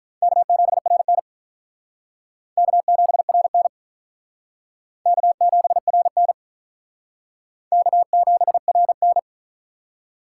Morse code, heard as a keyed tone, about 45 wpm, 700 Hz. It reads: K7RN K7RN K7RN K7RN